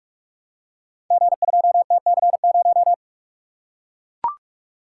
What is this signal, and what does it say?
Morse code, heard as a keyed tone, about 44 wpm, 700 Hz. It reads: G2TC0
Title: Code Groups - 5 Letter:Numbers - 45wpm